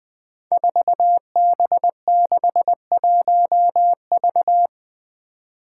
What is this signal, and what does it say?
Morse code, heard as a keyed tone, about 20 wpm, 700 Hz. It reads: 4B61V
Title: Five Challenging Letter-Number Combinations 20wpm